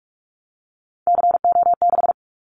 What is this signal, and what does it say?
Morse code, heard as a keyed tone, about 44 wpm, 700 Hz. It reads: CO6
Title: Callsign Prefixes for Top 100 DXCC - 45wpm